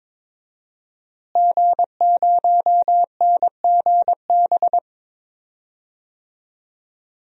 Morse code, heard as a keyed tone, about 21 wpm, 700 Hz. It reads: G0NGB
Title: DXCC-100 Call Signs - 5 Characters - Repeated 3x - 22wpm